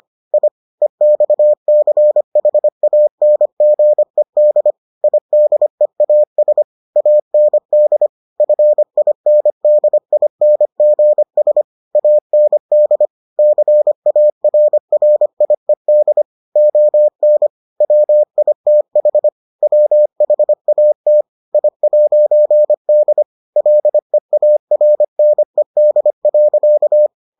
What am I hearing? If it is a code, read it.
Morse code, heard as a keyed tone, about 25 wpm, 600 Hz. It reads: I EXCHANGED IDEAS AND FINDINGS AND CARRIED ON WITH WHAT I'D LEARNED.